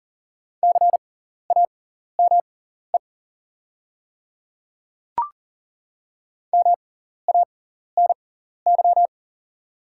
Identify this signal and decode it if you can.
Morse code, heard as a keyed tone, about 37 wpm, spaced as for 14 wpm, 690 Hz. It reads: CAME MANY
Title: 6x Character Spacing - Top 200 Words - 40wpm